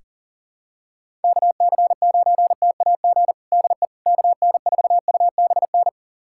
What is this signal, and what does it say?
Morse code, heard as a keyed tone, about 40 wpm, 700 Hz. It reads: KC9TAG DE KN4UBN
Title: USA Callsign DE Callsign - 6 Characters - 40wpm